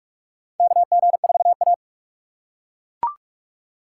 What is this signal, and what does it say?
Morse code, heard as a keyed tone, about 42 wpm, 700 Hz. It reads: KG4A
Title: USA Call Signs - 6x Character Spacing - 45wpm